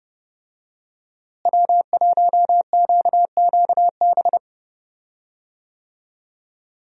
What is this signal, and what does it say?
Morse code, heard as a keyed tone, about 30 wpm, 700 Hz. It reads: W1QQB
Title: USA Call Signs - 5 Characters - Repeated 2x - 30wpm